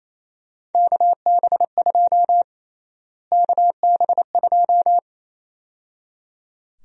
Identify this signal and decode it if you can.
Morse code, heard as a keyed tone, about 27 wpm, 700 Hz. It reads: KB2 KB2
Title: USA Call Sign Prefixes - Repeated 2x - 28wpm